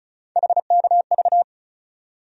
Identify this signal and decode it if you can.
Morse code, heard as a keyed tone, about 34 wpm, 700 Hz. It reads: HKV
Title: USA Call Sign Postfixes - 3 Letters 35wpm